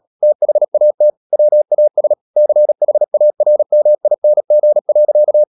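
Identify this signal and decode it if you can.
Morse code, heard as a keyed tone, about 37 wpm, 600 Hz. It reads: THAT WAS CHARMING.
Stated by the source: Morse code practice at 35–39 wpm sending English text of Ken Follett's novel Never, from CW med Gnister.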